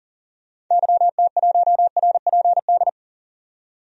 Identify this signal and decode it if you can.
Morse code, heard as a keyed tone, about 38 wpm, 700 Hz. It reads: YT1RPD
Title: DXCC-100 Call Signs - 6 Characters - 40wpm